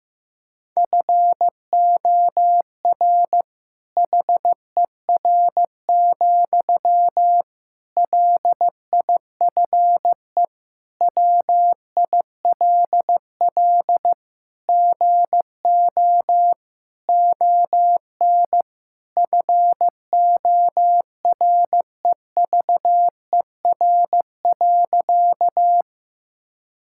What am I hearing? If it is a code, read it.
Morse code, heard as a keyed tone, about 15 wpm, 700 Hz. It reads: FOR HER, LIFE WILL GO ON FOREVER.
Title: Sentences from Top 1600 Words 15wpm